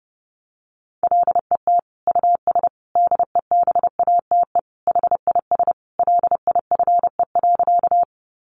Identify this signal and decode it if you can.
Morse code, heard as a keyed tone, about 30 wpm, 700 Hz. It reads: LET US DEBATE HIS LIFE.
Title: Sentences from Top 1100 Words 30wpm